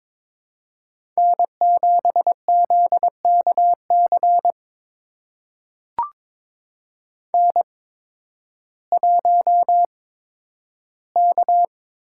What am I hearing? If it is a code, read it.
Morse code, heard as a keyed tone, about 22 wpm, spaced as for 12 wpm, 700 Hz. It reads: N7ZKC N1K
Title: USA Call Signs - 5 Characters - 8x Character Spacing - 22wpm